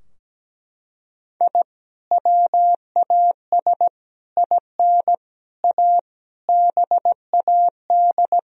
Morse code, heard as a keyed tone, about 17 wpm, 700 Hz. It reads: I WAS IN A BAD
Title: Sentences from Top 1000 Words - Review Part 2 - 17wpm